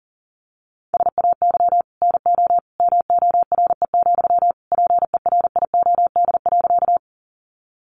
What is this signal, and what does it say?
Morse code, heard as a keyed tone, about 40 wpm, 690 Hz. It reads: SAY NO MORE, PERIOD.
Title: Sentences from Top 800 Words 40wpm